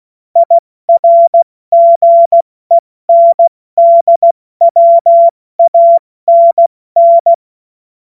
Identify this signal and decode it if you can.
Morse code, heard as a keyed tone, about 16 wpm, 680 Hz. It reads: IRGENDWANN
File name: Morsekurs / Witze / Witz 15. November 2022 70 F80